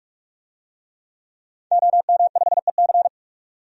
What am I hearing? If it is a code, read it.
Morse code, heard as a keyed tone, about 46 wpm, 700 Hz. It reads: OM5EC
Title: DXCC-100 Call Signs - 5 Characters - 45wpm